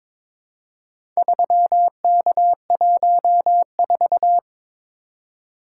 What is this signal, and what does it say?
Morse code, heard as a keyed tone, about 22 wpm, 700 Hz. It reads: 3K14
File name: practice / 22wpm / Four Letter-Number Combinations - 22wpm